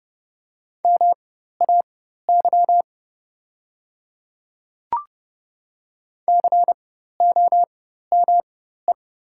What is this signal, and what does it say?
Morse code, heard as a keyed tone, about 30 wpm, spaced as for 14 wpm, 690 Hz. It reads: MAY COME
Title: 4x Character Spacing - Top 100 Words - 30wpm